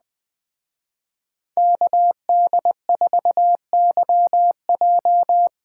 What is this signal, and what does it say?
Morse code, heard as a keyed tone, about 20 wpm, 700 Hz. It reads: KD4YJ
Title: Speed Racing - USA 2x2 Call Signs 17wpm